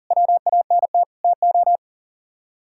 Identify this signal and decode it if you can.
Morse code, heard as a keyed tone, about 49 wpm, 700 Hz. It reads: WANT TO
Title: Speed-Racing - Sets of 5 Words 40wpm